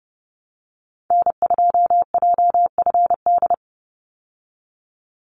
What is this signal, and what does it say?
Morse code, heard as a keyed tone, about 30 wpm, 700 Hz. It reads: N2JFD
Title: Call Signs Repeated 3x - 30wpm